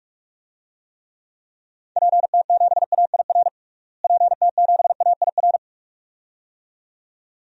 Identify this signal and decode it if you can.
Morse code, heard as a keyed tone, about 46 wpm, 700 Hz. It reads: PT7AIR PT7AIR
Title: DXCC-100 Call Signs - 6 Characters - Repeated 2x - 45wpm